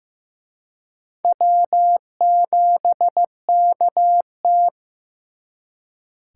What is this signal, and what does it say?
Morse code, heard as a keyed tone, about 18 wpm, 700 Hz. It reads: W7KT
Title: Speed Racing - USA 1x2 Call Signs 15wpm